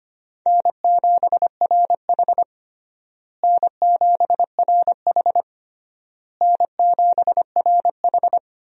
Morse code, heard as a keyed tone, about 25 wpm, 700 Hz. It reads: N7RH N7RH N7RH
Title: USA Call Signs - 4 Characters - Repeated 3x - 25wpm